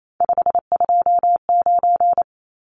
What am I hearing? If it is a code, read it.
Morse code, heard as a keyed tone, about 28 wpm, 700 Hz. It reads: 529
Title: Numbers - 3-Digit 28wpm